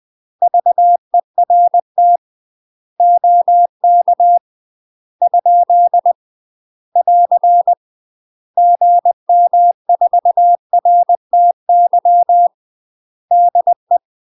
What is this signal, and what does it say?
Morse code, heard as a keyed tone, about 20 wpm, 700 Hz. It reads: VERT OK ? <AR> GM4RTY DE
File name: UK MASTER QSOs TRAINING / 20WPM / UK MASTER QSOs 0006